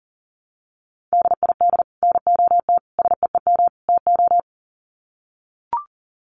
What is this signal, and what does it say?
Morse code, heard as a keyed tone, about 40 wpm, 700 Hz. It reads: DID NOT SEEM TO
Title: Sets of 4 Words 40wpm